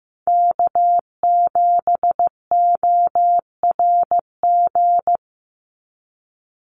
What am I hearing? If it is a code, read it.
Morse code, heard as a keyed tone, about 15 wpm, 700 Hz. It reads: K7ORG
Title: NA SOTA Chaser Callsigns 15wpm